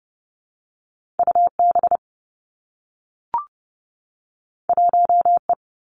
Morse code, heard as a keyed tone, about 30 wpm, 700 Hz. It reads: U B 1 E
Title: Two Challenging Letter-Number Combinations 30wpm